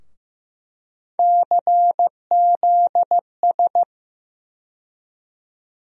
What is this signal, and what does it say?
Morse code, heard as a keyed tone, about 15 wpm, 700 Hz. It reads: CZS
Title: USA Call Sign Postfixes - 3 Letters 15wpm